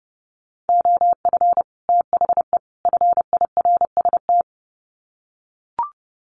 Morse code, heard as a keyed tone, about 30 wpm, 700 Hz. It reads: OF THE FIRST
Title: Sets of 3 Words 30wpm